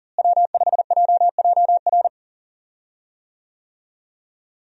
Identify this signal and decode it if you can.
Morse code, heard as a keyed tone, about 37 wpm, 700 Hz. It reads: W5JJR
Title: USA Call Signs - 5 Characters - Repeated 3x - 40wpm